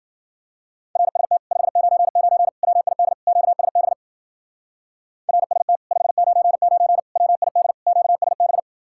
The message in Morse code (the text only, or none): RST 599 WID QSB RST 599 WID QSB